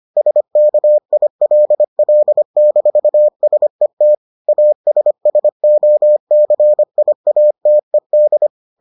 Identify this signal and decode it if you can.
Morse code, heard as a keyed tone, about 25 wpm, 600 Hz. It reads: SKILL-SET ASSOCIATED